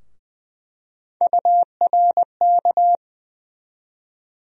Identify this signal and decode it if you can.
Morse code, heard as a keyed tone, about 20 wpm, 700 Hz. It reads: URK
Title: USA Call Sign Postfixes - 3 Letters 20wpm